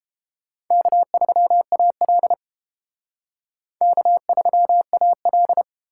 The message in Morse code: K3AL K3AL

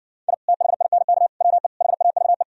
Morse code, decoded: I THINK WE HAVE

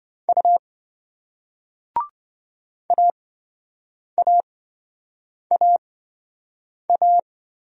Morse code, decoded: U AAAA

